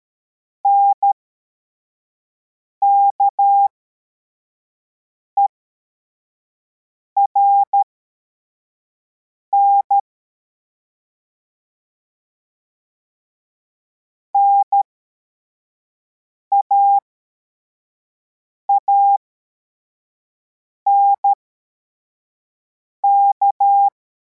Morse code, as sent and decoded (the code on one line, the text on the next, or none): -. -.- . .-. -. / -. .- .- -. -.-
NKERN NAANK